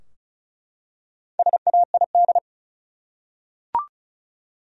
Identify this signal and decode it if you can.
Morse code, heard as a keyed tone, about 36 wpm, 700 Hz. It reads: SAID
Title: Top 100 Words 35wpm